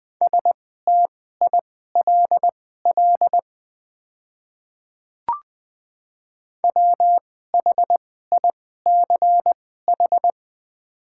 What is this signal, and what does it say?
Morse code, heard as a keyed tone, about 20 wpm, spaced as for 15 wpm, 700 Hz. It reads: STILL WHICH
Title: 2x Character Spacing - Top 500 Words - 20wpm